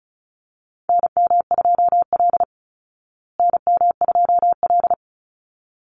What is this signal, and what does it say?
Morse code, heard as a keyed tone, about 34 wpm, 700 Hz. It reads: NM2L NM2L
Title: Call Signs Repeated 3x - 35wpm